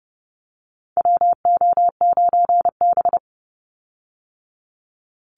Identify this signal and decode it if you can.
Morse code, heard as a keyed tone, about 30 wpm, 700 Hz. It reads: WO9B
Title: Call Signs Repeated 3x - 30wpm